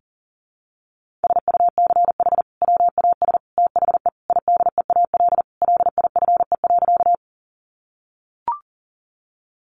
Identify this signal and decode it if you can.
Morse code, heard as a keyed tone, about 40 wpm, 700 Hz. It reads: SUCH WAS THE IDEAL LIFE.
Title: Sentences from Top 1400 Words 40wpm